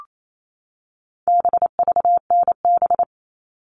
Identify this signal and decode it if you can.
Morse code, heard as a keyed tone, about 28 wpm, 700 Hz. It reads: BVNB
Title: All Cut Numbers - 4-Digit 28wpm